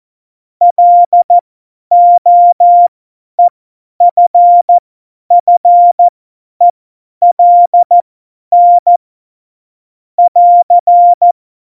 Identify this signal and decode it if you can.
Morse code, heard as a keyed tone, about 14 wpm, spaced as for 10 wpm, 700 Hz. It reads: LOEFFELN <AR>
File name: Morsekurs / Witze / Witz 07 April 2020, 50 F70